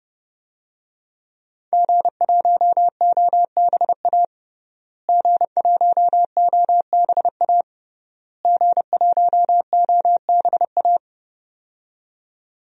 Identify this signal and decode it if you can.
Morse code, heard as a keyed tone, about 32 wpm, 700 Hz. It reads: G1OBA G1OBA G1OBA
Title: DXCC-100 Call Signs - 5 Characters - Repeated 3x - 30wpm